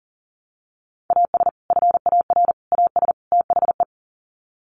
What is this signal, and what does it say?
Morse code, heard as a keyed tone, about 40 wpm, 700 Hz. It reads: AS FAR AS THE
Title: Sets of 4 Words 40wpm